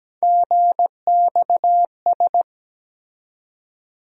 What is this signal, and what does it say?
Morse code, heard as a keyed tone, about 17 wpm, 700 Hz. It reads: GXS
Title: USA Call Sign Postfixes - 3 Letters 17wpm